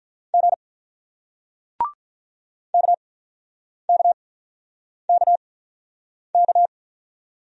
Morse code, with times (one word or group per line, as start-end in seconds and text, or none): 0.3-0.5 G
2.7-6.7 KKKK